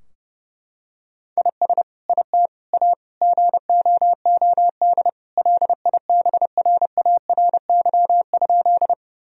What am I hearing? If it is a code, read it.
Morse code, heard as a keyed tone, about 32 wpm, 700 Hz. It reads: IS IT A GOOD LIBRARY?
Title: Sentences from Top 1000 Words - Review Part 4 - 30wpm